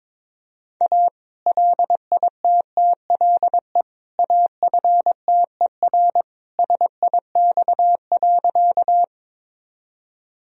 Morse code, heard as a keyed tone, about 22 wpm, 700 Hz. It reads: A LITTLE AFTER SIX.
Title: Sentences from Top 1000 Words - Review Part 2 - 22wpm